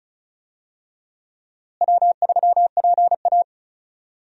Morse code, heard as a keyed tone, about 36 wpm, 700 Hz. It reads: W3PA
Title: USA Call Signs - 4 Characters - Repeated 3x - 35wpm